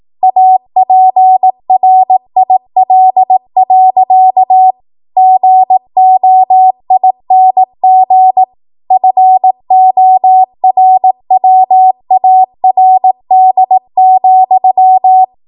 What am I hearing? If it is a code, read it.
Morse code, heard as a keyed tone, about 18 wpm, 750 Hz. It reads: APRIL. GOING FORWARD,
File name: Morse / 18WPM